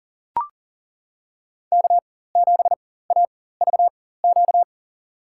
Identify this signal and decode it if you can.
Morse code, heard as a keyed tone, about 40 wpm, spaced as for 19 wpm, 700 Hz. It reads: K7AVQ
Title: USA Call Signs - 5 Characters - 4x Character Spacing - 40wpm